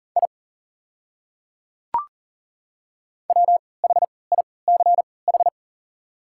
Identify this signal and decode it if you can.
Morse code, heard as a keyed tone, about 40 wpm, spaced as for 23 wpm, 690 Hz. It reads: I WHICH
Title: 3x Character Spacing - Top 100 Words - 40wpm